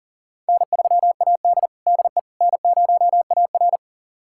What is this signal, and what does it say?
Morse code, heard as a keyed tone, about 40 wpm, 700 Hz. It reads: N3AD DE N0AR
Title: USA Callsign DE Callsign - 4 Characters - 40wpm